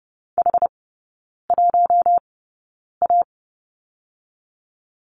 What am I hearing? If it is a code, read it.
Morse code, heard as a keyed tone, about 30 wpm, spaced as for 9 wpm, 690 Hz. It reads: H1A
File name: Lesson 16e - Letter c - Three Characters 30wpm